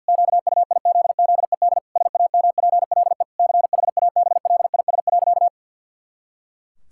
Morse code, heard as a keyed tone, about 52 wpm, 700 Hz. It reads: QUIZZED SAMPLE CHABLIS.
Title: Pangram Challenge v2 - 50wpm